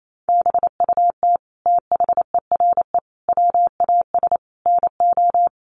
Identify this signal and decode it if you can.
Morse code, heard as a keyed tone, about 28 wpm, 700 Hz. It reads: BUT THERE WAS NO
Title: Sentences from Top 1400 Words 28wpm